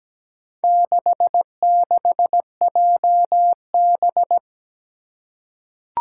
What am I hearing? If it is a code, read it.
Morse code, heard as a keyed tone, about 17 wpm, 700 Hz. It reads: 66JB
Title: Four Challenging Letter-Number Combinations 17wpm